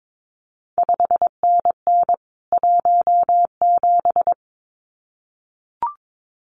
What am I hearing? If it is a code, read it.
Morse code, heard as a keyed tone, about 22 wpm, 700 Hz. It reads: 5NN 17
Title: CQ WW DX Contest Exchange Encore - 22wpm